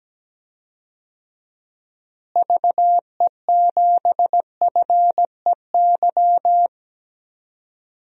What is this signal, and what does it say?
Morse code, heard as a keyed tone, about 17 wpm, 700 Hz. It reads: VE7FEY
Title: DXCC-100 Call Signs - 6 Characters - Repeated 3x - 17wpm